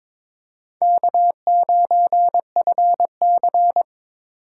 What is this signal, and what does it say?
Morse code, heard as a keyed tone, about 22 wpm, 700 Hz. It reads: K9FC
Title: USA Call Signs - 4 Characters - 3x Character Spacing - 22wpm